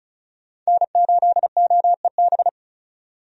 Morse code, heard as a keyed tone, about 36 wpm, 700 Hz. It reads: N8OEB
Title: USA Call Signs - 5 Characters - 35wpm